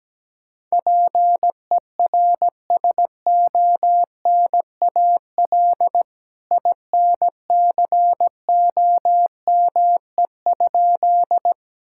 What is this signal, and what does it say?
Morse code, heard as a keyed tone, about 17 wpm, 700 Hz. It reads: PERSONAL INCOME?
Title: Sentences from Top 1000 Words - Review Part 5 - 17wpm